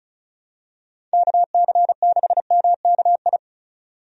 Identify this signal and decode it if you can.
Morse code, heard as a keyed tone, about 35 wpm, 700 Hz. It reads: KC6MKI
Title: USA Call Signs - 6 Characters - 35wpm